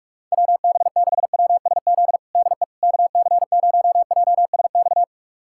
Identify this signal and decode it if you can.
Morse code, heard as a keyed tone, about 44 wpm, 700 Hz. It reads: WB6WSZ DE KC0JSX
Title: USA Callsign DE Callsign 45wpm